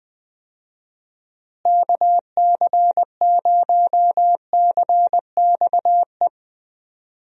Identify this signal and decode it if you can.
Morse code, heard as a keyed tone, about 20 wpm, 700 Hz. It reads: KC0CXE
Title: USA Call Signs - 6 Characters - Repeated 2x - 20wpm